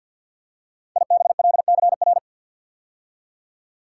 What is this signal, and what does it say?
Morse code, heard as a keyed tone, about 47 wpm, 700 Hz. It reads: IBLCR
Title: Code Groups - 5 Letter:Numbers - 50wpm